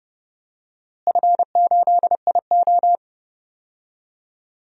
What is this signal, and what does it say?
Morse code, heard as a keyed tone, about 30 wpm, 700 Hz. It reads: F8IO
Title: DXCC-100 Call Signs - 4 Characters - Repeated 3x - 30wpm